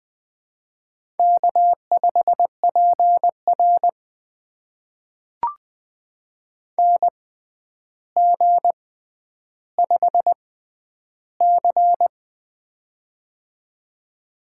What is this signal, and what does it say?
Morse code, heard as a keyed tone, about 20 wpm, spaced as for 10 wpm, 700 Hz. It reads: K5PR NG5C
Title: USA Call Signs - 4 Characters - 6x Character Spacing - 20wpm